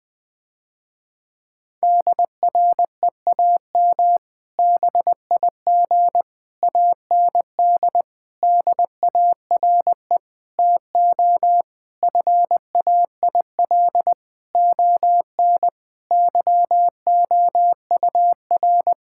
Morse code, heard as a keyed tone, about 20 wpm, 700 Hz. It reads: DREAM BIG AND DARE TO FAIL ON YOUR
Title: Difficult Sentences with Hints - 20wpm